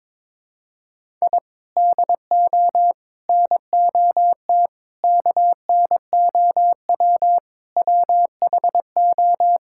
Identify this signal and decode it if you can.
Morse code, heard as a keyed tone, about 22 wpm, 700 Hz. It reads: I DO NOT KNOW WHO
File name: practice / 22wpm / Build A Phrase - 5-Word Phrases - 22wpm